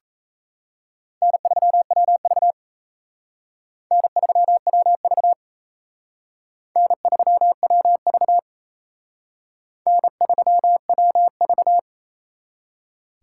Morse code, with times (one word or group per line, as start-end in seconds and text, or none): 1.2-2.5 N3WV
3.9-5.3 N3WV
6.8-8.4 N3WV
9.9-11.8 N3WV